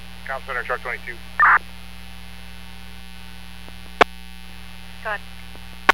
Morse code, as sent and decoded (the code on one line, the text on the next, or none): none